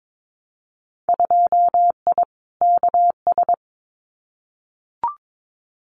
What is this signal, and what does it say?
Morse code, heard as a keyed tone, about 22 wpm, 700 Hz. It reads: 2I KS
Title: Winter Field Day Exchanges Encore 22wpm